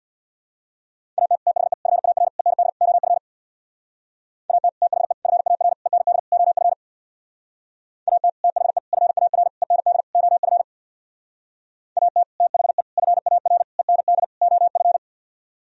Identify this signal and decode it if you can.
Morse code, heard as a keyed tone, about 62 wpm, 700 Hz. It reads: AT THE FAR END OF AT THE FAR END OF AT THE FAR END OF AT THE FAR END OF
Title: Speed-Racing - Sets of 5 Words 50wpm